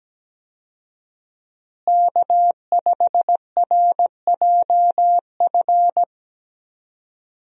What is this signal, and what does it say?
Morse code, heard as a keyed tone, about 17 wpm, 700 Hz. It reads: K5RJF
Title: USA Call Signs - 5 Characters - Repeated 2x - 17wpm